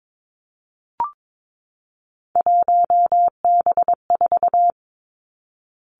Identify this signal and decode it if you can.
Morse code, heard as a keyed tone, about 22 wpm, 700 Hz. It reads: 1B4
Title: Three Challenging Letter-Number Combinations 22wpm